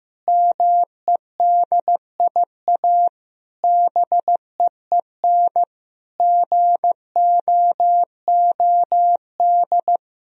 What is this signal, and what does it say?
Morse code, heard as a keyed tone, about 15 wpm, 700 Hz. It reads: MEDIA BEEN GOOD
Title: Sentences from Top 1000 Words - Review Part 4 - 15wpm